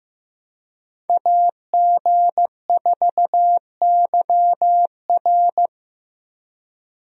Morse code, heard as a keyed tone, about 15 wpm, 700 Hz. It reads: AG4YR
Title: USA Call Signs - 5 Characters - Repeated 2x - 15wpm